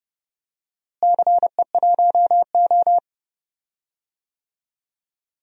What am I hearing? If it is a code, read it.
Morse code, heard as a keyed tone, about 30 wpm, 700 Hz. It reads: CE1O
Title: DXCC-100 Call Signs - 4 Characters - Repeated 3x - 30wpm